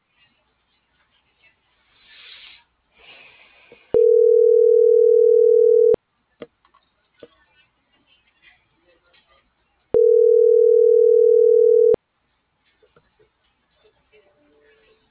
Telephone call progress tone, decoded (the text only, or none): ringback tone